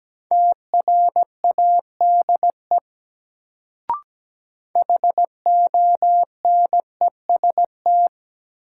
Morse code, heard as a keyed tone, about 17 wpm, 700 Hz. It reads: TRADE HONEST
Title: Sets of 5 Words Encore - New Word Drill - 17wpm